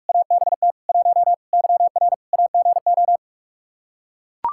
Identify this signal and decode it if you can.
Morse code, heard as a keyed tone, about 45 wpm, 700 Hz. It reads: ABT 1 YR AGO
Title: Practice QSO - Exchange 3 - More Info - 45wpm